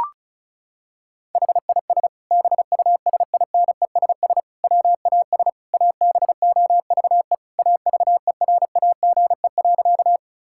Morse code, heard as a keyed tone, about 36 wpm, 700 Hz. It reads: HIS BUSINESS WAS ABOVE AVERAGE.
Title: Sentences from Top 1000 Words - Review Part 6 - 35wpm